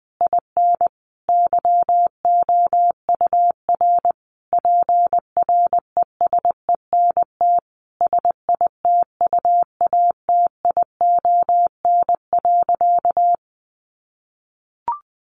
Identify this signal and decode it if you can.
Morse code, heard as a keyed tone, about 20 wpm, 700 Hz. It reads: IN YOUR PRESENT SITUATION.
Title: Sentences from Top 1200 Words 20wpm